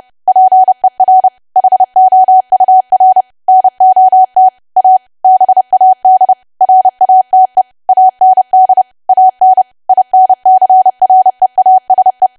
Morse code, heard as a keyed tone, about 30 wpm, 750 Hz. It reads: PER HOUR NOT A BAD RATE AND AN INCREASE
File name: cwa / W1AW-Level3 / 30WPM